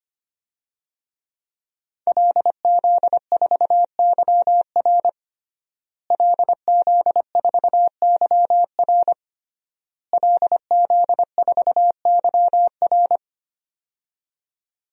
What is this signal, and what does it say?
Morse code, heard as a keyed tone, about 25 wpm, 700 Hz. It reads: LZ4YR LZ4YR LZ4YR
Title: DXCC-100 Call Signs - 5 Characters - Repeated 3x - 25wpm